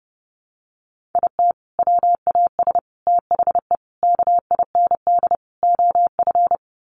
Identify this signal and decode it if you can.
Morse code, heard as a keyed tone, about 30 wpm, 700 Hz. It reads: IT WAS THE KIND OF
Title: Sets of 5 Words Encore2 30wpm